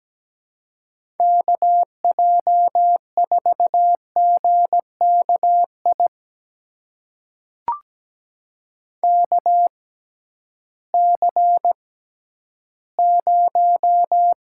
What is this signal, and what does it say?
Morse code, heard as a keyed tone, about 17 wpm, spaced as for 11 wpm, 700 Hz. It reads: KJ4GKI KC0
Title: USA Call Signs - 6 Characters - 6x Character Spacing - 17wpm